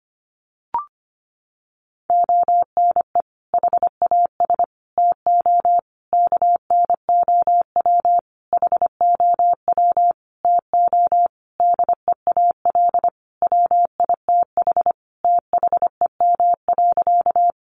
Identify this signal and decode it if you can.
Morse code, heard as a keyed tone, about 25 wpm, 700 Hz. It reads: ONE HAS TO KNOW HOW TO DEAL WITH THEM.
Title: Sentences from Top 800 Words 25wpm